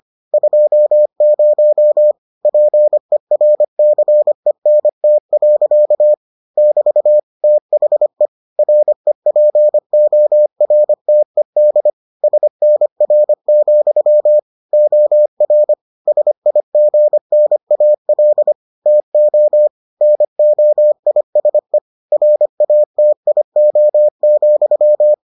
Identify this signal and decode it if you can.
Morse code, heard as a keyed tone, about 25 wpm, 600 Hz. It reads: 20 PERCENT. <BT> THE REPORTED SNR, OR SIGNAL TO NOISE RATIO,